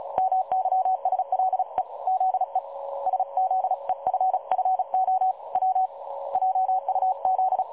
Morse code, heard as a keyed tone, about 36 wpm, 750 Hz. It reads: MYSLE ZE SZEFFO W JUZ
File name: dlazaawansowanych / nagraniazpasma / websdr recording 3565.0kHz